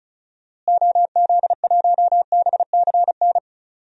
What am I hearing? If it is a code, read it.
Morse code, heard as a keyed tone, about 34 wpm, 700 Hz. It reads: OZ1BCN
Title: DXCC-100 Call Signs - 6 Characters - Repeated 3x - 35wpm